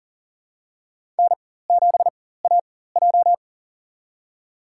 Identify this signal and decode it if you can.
Morse code, heard as a keyed tone, about 38 wpm, spaced as for 18 wpm, 690 Hz. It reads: N7AJ
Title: USA Call Signs - 4x Character Spacing - 40wpm